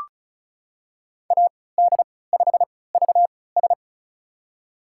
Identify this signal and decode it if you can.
Morse code, heard as a keyed tone, about 36 wpm, spaced as for 20 wpm, 700 Hz. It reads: AD5VS
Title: USA Call Signs - 5 Characters - 3x Character Spacing - 35wpm